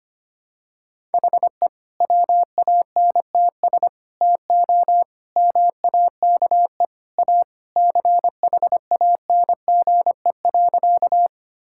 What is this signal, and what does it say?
Morse code, heard as a keyed tone, about 25 wpm, 700 Hz. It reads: HE WANTS TO MAKE A CHANGE.
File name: practice / 25wpm / Sentences with 3-Word Phrases - 25wpm